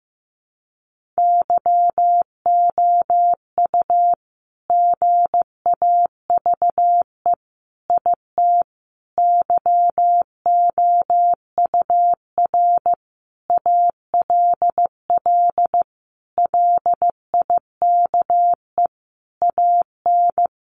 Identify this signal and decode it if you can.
Morse code, heard as a keyed tone, about 15 wpm, 700 Hz. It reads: YOU GAVE IT YOUR ALL LIKE AN
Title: Sentences from Top 1500 Words 15wpm